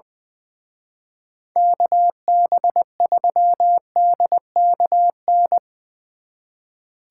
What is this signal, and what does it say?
Morse code, heard as a keyed tone, about 22 wpm, 700 Hz. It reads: KB3DKN
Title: Speed Racing - USA 2x3 Call Signs 20wpm